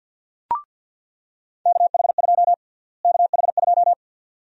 Morse code, heard as a keyed tone, about 49 wpm, 700 Hz. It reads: KH2 KH2
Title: USA Call Sign Prefixes - Repeated 2x - 50wpm